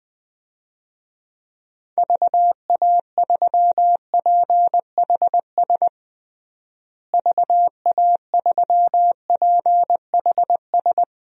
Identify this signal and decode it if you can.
Morse code, heard as a keyed tone, about 21 wpm, 700 Hz. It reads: VA3PHS VA3PHS
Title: DXCC-100 Call Signs - 6 Characters - Repeated 3x - 20wpm